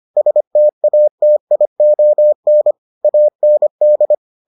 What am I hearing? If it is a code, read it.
Morse code, heard as a keyed tone, about 25 wpm, 600 Hz. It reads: STATION AND